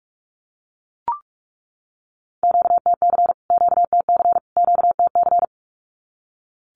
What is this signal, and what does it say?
Morse code, heard as a keyed tone, about 44 wpm, 700 Hz. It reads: QTC QTC QTC
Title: Q-Codes Repeated 3x 45wpm